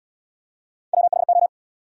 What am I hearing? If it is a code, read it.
Morse code, heard as a keyed tone, about 78 wpm, 700 Hz. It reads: WHO